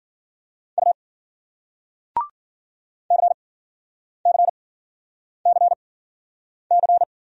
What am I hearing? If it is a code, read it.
Morse code, heard as a keyed tone, about 52 wpm, spaced as for 9 wpm, 700 Hz. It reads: U CCCC